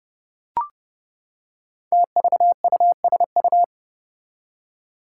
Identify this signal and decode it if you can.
Morse code, heard as a keyed tone, about 32 wpm, 700 Hz. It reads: TVUSU
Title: Five Challenging Letter-Number Combinations 30wpm